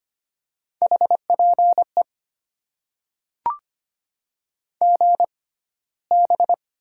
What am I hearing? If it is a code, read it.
Morse code, heard as a keyed tone, about 25 wpm, spaced as for 15 wpm, 700 Hz. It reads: HPE GB
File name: Top 90 QSO Elements - 6x Character Spacing - 25wpm